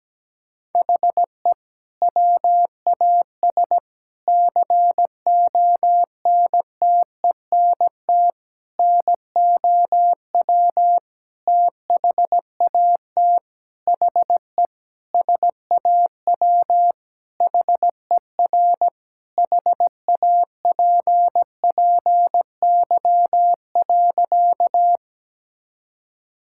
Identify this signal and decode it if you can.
Morse code, heard as a keyed tone, about 17 wpm, 700 Hz. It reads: HE WAS CONTENT NOW THAT HE SAW HER HAPPY.
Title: Sentences from Top 1000 Words - Review Part 6 - 17wpm